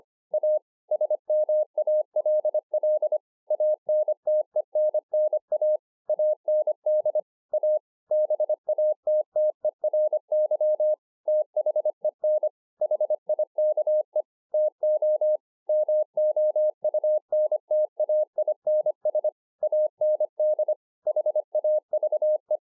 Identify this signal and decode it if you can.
Morse code, heard as a keyed tone, about 25 wpm, 610 Hz. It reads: A SMALL ANTENNA AND A BATTERY THEN HIKE TO MOUNTAINS AND HAVE